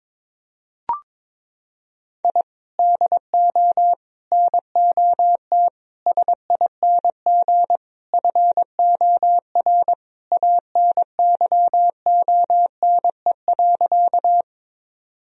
Morse code, heard as a keyed tone, about 22 wpm, 700 Hz. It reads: I DO NOT SING FOR ANYONE.
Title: Sentences from Top 1000 Words - Review Part 2 - 22wpm